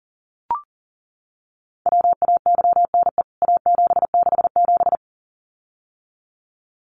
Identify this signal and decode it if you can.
Morse code, heard as a keyed tone, about 40 wpm, 700 Hz. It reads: WAYNE A767
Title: CWT Exchange 40wpm